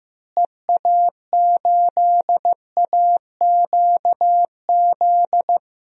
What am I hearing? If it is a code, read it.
Morse code, heard as a keyed tone, about 15 wpm, 700 Hz. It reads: EA8AQZ
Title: DXCC-100 Call Signs - 6 Characters - Repeated 2x - 15wpm